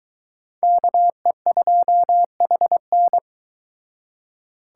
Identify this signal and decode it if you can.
Morse code, heard as a keyed tone, about 22 wpm, 700 Hz. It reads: KE2HN